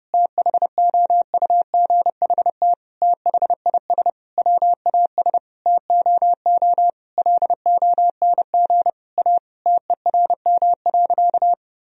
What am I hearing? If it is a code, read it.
Morse code, heard as a keyed tone, about 30 wpm, 700 Hz. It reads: THOUGHT THIS WAS TOO LONG A TERM.
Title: Sentences from Top 1000 Words - Review Part 4 - 30wpm